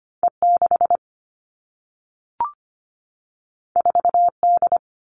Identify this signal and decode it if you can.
Morse code, heard as a keyed tone, about 25 wpm, 700 Hz. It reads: E6 4D
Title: All Cut Numbers - 2-Digit 25wpm